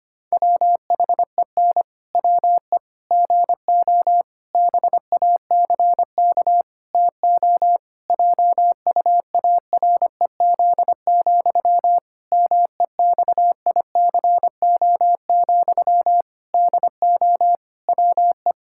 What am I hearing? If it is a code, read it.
Morse code, heard as a keyed tone, about 25 wpm, 700 Hz. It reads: WHEN WE GO BACK TO JUAREZ, MEXICO, DO WE
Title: Pangram Challenge v2 - 25wpm